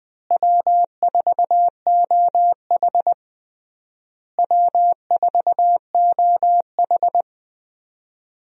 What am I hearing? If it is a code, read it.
Morse code, heard as a keyed tone, about 20 wpm, 700 Hz. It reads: W4OH W4OH
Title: USA Call Signs - 4 Characters - Repeated 3x - 20wpm